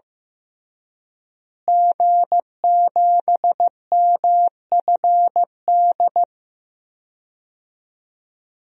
Morse code, heard as a keyed tone, about 15 wpm, 700 Hz. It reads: G7MFD
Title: DXCC-100 Call Signs - 5 Characters - Repeated 3x - 15wpm